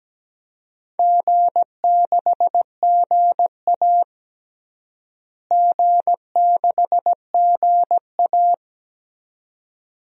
Morse code, heard as a keyed tone, about 17 wpm, 700 Hz. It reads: G6GA G6GA
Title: DXCC-100 Call Signs - 4 Characters - Repeated 2x - 17wpm